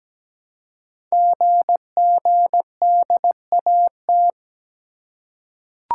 Morse code, heard as a keyed tone, about 17 wpm, 700 Hz. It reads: GGDAT
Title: Code Groups - 5 Letters - 17wpm